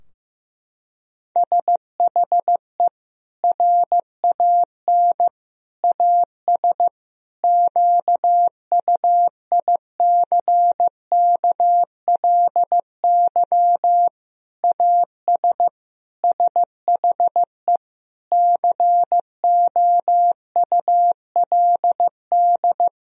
Morse code, heard as a keyed tone, about 15 wpm, 700 Hz. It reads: SHE RAN AS QUICKLY AS SHE COULD
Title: Sentences from Top 1000 Words - Review Part 4 - 15wpm